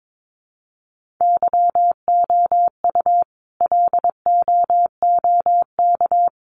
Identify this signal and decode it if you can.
Morse code, heard as a keyed tone, about 22 wpm, 700 Hz. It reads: YOU LOOK
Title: Sentences from Top 1500 Words 22wpm